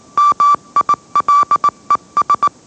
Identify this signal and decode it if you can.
Morse code, heard as a keyed tone, about 18 wpm, 1160 Hz. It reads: MILES